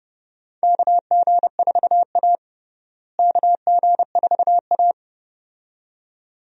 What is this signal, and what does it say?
Morse code, heard as a keyed tone, about 31 wpm, 700 Hz. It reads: KG4A KG4A
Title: USA Call Signs - 4 Characters - Repeated 2x - 30wpm